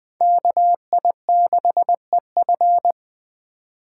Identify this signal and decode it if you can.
Morse code, heard as a keyed tone, about 20 wpm, 700 Hz. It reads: KI6EF
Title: Speed Racing - USA 2x2 Call Signs 17wpm